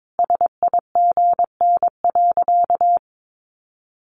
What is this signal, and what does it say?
Morse code, heard as a keyed tone, about 22 wpm, 700 Hz. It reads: SIGN.
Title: Sentences from Top 1100 Words 22wpm